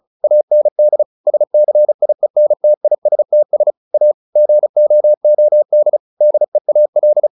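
Morse code, heard as a keyed tone, about 34 wpm, 600 Hz. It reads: AND SCIENTISTS A GOOD DEAL
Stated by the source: Morse practice at 35 wpm from CW med Gnister, sending Ken Follett's novel Never in English.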